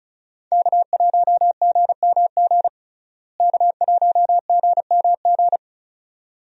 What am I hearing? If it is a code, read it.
Morse code, heard as a keyed tone, about 34 wpm, 700 Hz. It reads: K1GMG K1GMG
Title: USA Call Signs - 5 Characters - Repeated 2x - 35wpm